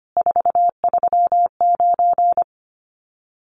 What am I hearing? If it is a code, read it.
Morse code, heard as a keyed tone, about 25 wpm, 700 Hz. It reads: 439
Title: Common QSO Phrases 25wpm